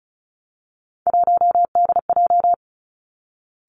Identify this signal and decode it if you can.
Morse code, heard as a keyed tone, about 36 wpm, 700 Hz. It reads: 1DJ
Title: Three Challenging Letter-Number Combinations 35wpm